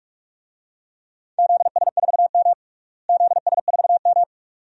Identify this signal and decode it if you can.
Morse code, heard as a keyed tone, about 43 wpm, 700 Hz. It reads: ZS4M ZS4M
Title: DXCC-100 Call Signs - 4 Characters - Repeated 2x - 45wpm